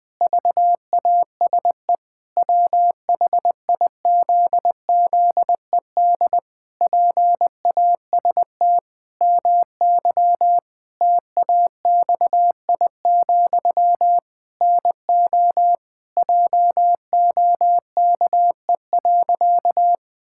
Morse code, heard as a keyed tone, about 20 wpm, 700 Hz. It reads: VASE WHIZZED PAST MY TAXI, NO JOKE.
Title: Pangram Challenge v2 - 20wpm